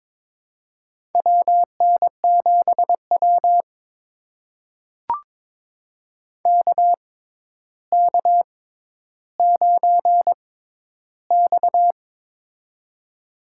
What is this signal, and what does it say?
Morse code, heard as a keyed tone, about 22 wpm, spaced as for 11 wpm, 700 Hz. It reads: WN7W KK9X
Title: USA Call Signs - 4 Characters - 6x Character Spacing - 22wpm